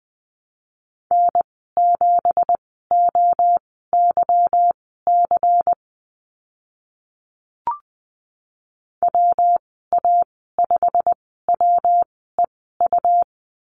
Morse code, heard as a keyed tone, about 20 wpm, spaced as for 14 wpm, 700 Hz. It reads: N7OYC WA5WEU
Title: 186 Lesson 30g - Letter b - Call Signs 20wpm